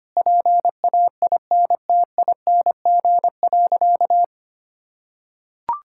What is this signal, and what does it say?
Morse code, heard as a keyed tone, about 25 wpm, 700 Hz. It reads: PAINTING.
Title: Sentences from Top 1000 Words - Review Part 6 - 25wpm